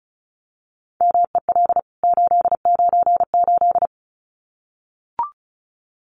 Morse code, heard as a keyed tone, about 34 wpm, 700 Hz. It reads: MEL 898
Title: CWT Exchange 35wpm